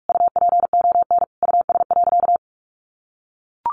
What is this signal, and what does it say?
Morse code, heard as a keyed tone, about 44 wpm, 700 Hz. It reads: UPON US.
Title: Sentences from Top 1500 Words 45wpm